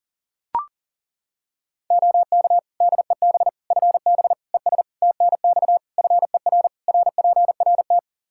Morse code, heard as a keyed tone, about 40 wpm, 700 Hz. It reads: OK DEB FB ES TNX FER RPRT
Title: Practice QSO - Exchange 2 - Info - 40wpm